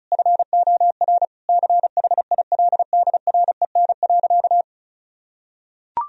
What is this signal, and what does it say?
Morse code, heard as a keyed tone, about 36 wpm, 700 Hz. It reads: FOR CHILDREN.
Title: Sentences from Top 1000 Words - Review Part 4 - 35wpm